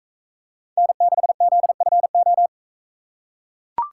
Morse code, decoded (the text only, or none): N6ZFO